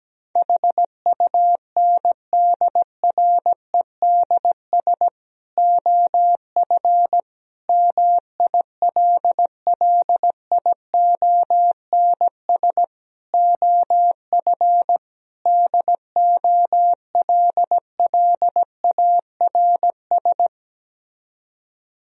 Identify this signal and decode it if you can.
Morse code, heard as a keyed tone, about 17 wpm, 700 Hz. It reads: HUNDREDS OF MILLIONS OF DOLLARS
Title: Speed-Racing - Sets of 5 Words 17wpm